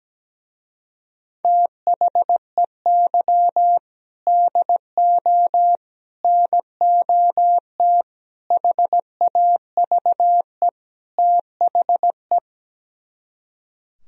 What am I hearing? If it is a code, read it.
Morse code, heard as a keyed tone, about 17 wpm, 700 Hz. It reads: THEY DO NOT HAVE THE
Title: Speed-Racing - Sets of 5 Words 17wpm